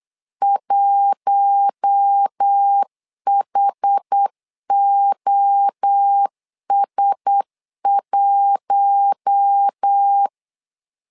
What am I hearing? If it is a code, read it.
Morse code, heard as a keyed tone, about 8 wpm, 800 Hz. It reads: 1HOS1
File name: LEZIONI / 4 Q1S 8wpm 2710